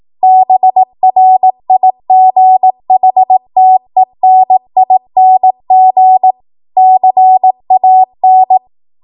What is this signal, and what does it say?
Morse code, heard as a keyed tone, about 18 wpm, 750 Hz. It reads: BRIGHTENING CAN
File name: Morse / 18WPM